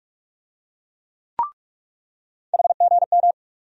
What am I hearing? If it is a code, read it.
Morse code, heard as a keyed tone, about 42 wpm, 700 Hz. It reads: HGM
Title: USA Call Sign Postfixes - 3 Letters 45wpm